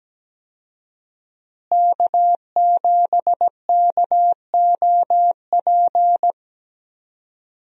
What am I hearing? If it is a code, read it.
Morse code, heard as a keyed tone, about 17 wpm, 700 Hz. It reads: K7KOP
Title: USA Call Signs - 5 Characters - Repeated 2x - 17wpm